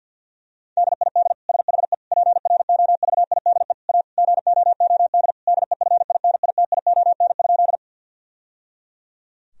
Morse code, heard as a keyed tone, about 48 wpm, 700 Hz. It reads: DID SHE PROVIDE A GOOD DEFINITION?
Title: Sentences from Top 1000 Words - Review Part 4 - 50wpm